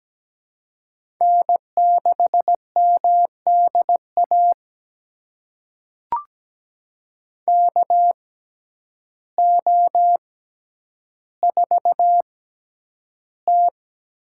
Morse code, heard as a keyed tone, about 17 wpm, spaced as for 9 wpm, 700 Hz. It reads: N6MDA KO4T
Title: USA Call Signs - 6x Character Spacing - 17wpm